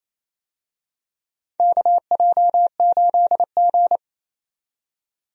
Morse code, heard as a keyed tone, about 28 wpm, 690 Hz. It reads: KJ8G
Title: USA Call Signs - 4 Characters - 28wpm